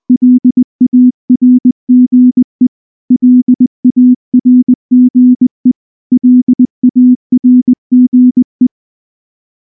Morse code, heard as a keyed tone, about 19 wpm, 250 Hz. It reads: LARGE LARGE LARGE
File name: morse / morse-training-20WPM-6